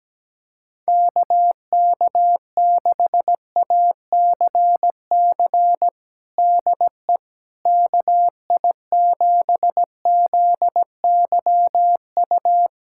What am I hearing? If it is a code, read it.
Morse code, heard as a keyed tone, about 17 wpm, 700 Hz. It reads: KK6ACC DE KI7ZYU
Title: USA Callsign DE Callsign - 6 Characters - 17wpm